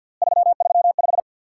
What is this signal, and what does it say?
Morse code, heard as a keyed tone, about 50 wpm, 690 Hz. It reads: 335
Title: Worked All Europe DX Contest - QTC Exchange with Repeats - 50wpm